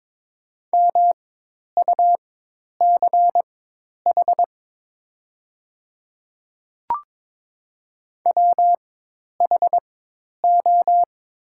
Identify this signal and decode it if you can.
Morse code, heard as a keyed tone, about 22 wpm, spaced as for 10 wpm, 700 Hz. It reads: MUCH WHO
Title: 4x Character Spacing - Top 300 Words - 22wpm